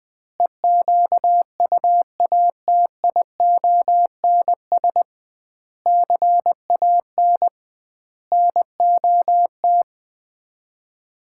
Morse code, heard as a keyed tone, about 20 wpm, 700 Hz. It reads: EQUATIONS CAN NOT
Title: 2x Word-Spacing - Sets of 5 Words - 20wpm